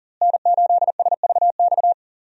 Morse code, heard as a keyed tone, about 40 wpm, 700 Hz. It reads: N8SVX
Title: USA Call Signs - 5 Characters - Repeated 2x - 40wpm